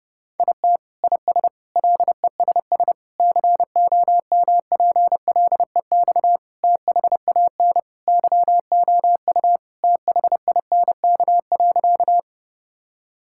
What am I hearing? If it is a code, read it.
Morse code, heard as a keyed tone, about 32 wpm, 700 Hz. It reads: IT IS LESS COMPLEX THAN YOU THINK.
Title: Sentences from Top 1000 Words - Review Part 6 - 30wpm